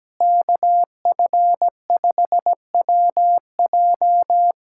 Morse code, heard as a keyed tone, about 17 wpm, 700 Hz. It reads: KF5WJ
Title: K1USN SST QSO Practice - 17wpm